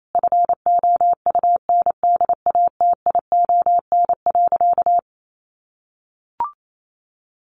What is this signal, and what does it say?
Morse code, heard as a keyed tone, about 28 wpm, 700 Hz. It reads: FOUNDATION.
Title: Sentences from Top 1200 Words 28wpm